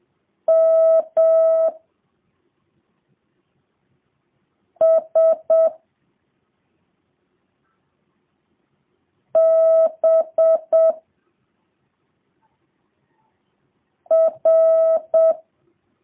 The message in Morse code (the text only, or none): M S B R